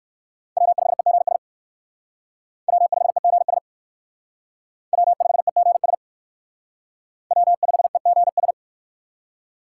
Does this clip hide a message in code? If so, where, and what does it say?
Yes: Morse code from 0.6 to 8.5 s, heard as a keyed tone, about 59 wpm, 700 Hz. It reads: W5EGS W5EGS W5EGS W5EGS